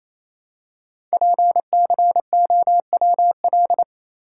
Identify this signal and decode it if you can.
Morse code, heard as a keyed tone, about 29 wpm, 700 Hz. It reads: PCOWL
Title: Code Groups - 5 Letters - 28wpm